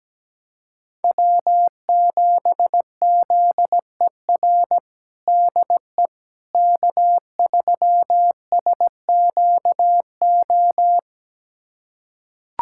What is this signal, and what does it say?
Morse code, heard as a keyed tone, about 17 wpm, 700 Hz. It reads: W7ZER DE K3SQO
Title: USA Callsign DE Callsign - 5 Characters - 17wpm